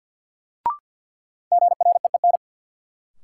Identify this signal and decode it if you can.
Morse code, heard as a keyed tone, about 52 wpm, 700 Hz. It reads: GREEN
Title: Sentences from Top 500 Words - New Word Drill - 50wpm